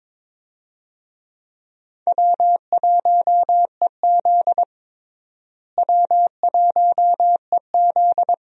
Morse code, heard as a keyed tone, about 22 wpm, 700 Hz. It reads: W1EZ W1EZ
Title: USA Call Signs - 4 Characters - Repeated 2x - 22wpm